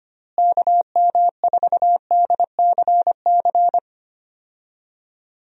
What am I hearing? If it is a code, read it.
Morse code, heard as a keyed tone, about 25 wpm, 700 Hz. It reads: KM4DCC
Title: USA Call Signs - 6 Characters - Repeated 3x - 25wpm